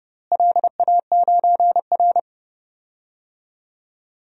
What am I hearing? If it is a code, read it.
Morse code, heard as a keyed tone, about 30 wpm, 700 Hz. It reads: LA9R
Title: DXCC-100 Call Signs - 4 Characters - Repeated 2x - 30wpm